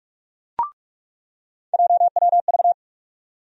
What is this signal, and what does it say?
Morse code, heard as a keyed tone, about 46 wpm, 700 Hz. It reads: JWV
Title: USA Call Sign Postfixes - 3 Letters 45wpm